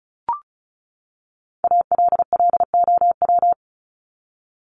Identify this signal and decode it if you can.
Morse code, heard as a keyed tone, about 34 wpm, 700 Hz. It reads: ALLOW
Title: Top 501-750 Words 35wpm